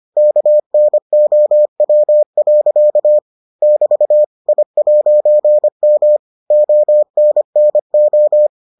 Morse code, heard as a keyed tone, about 25 wpm, 600 Hz. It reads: KNOW. <BT> I'M ONNO